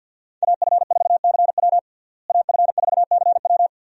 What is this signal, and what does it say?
Morse code, heard as a keyed tone, about 47 wpm, 700 Hz. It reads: AF4CW AF4CW
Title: USA Call Signs - 5 Characters - Repeated 2x - 50wpm